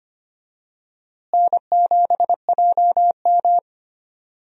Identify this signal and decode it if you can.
Morse code, heard as a keyed tone, about 25 wpm, 700 Hz. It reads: N7JM
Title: USA Call Signs - 4 Characters - Repeated 3x - 25wpm